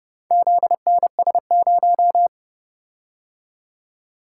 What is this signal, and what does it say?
Morse code, heard as a keyed tone, about 29 wpm, 700 Hz. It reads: ZNS0